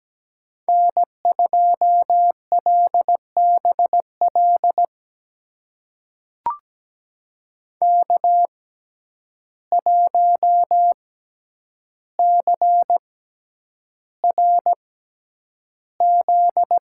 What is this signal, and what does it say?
Morse code, heard as a keyed tone, about 17 wpm, spaced as for 9 wpm, 700 Hz. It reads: N2LBL K1CRZ
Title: USA Call Signs - 5 Characters - 6x Character Spacing - 17wpm